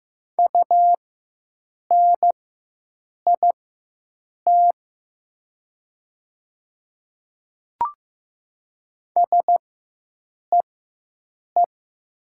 Morse code, heard as a keyed tone, about 15 wpm, spaced as for 7 wpm, 700 Hz. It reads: UNIT SEE
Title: 4x Character Spacing - Top 500 Words - 15wpm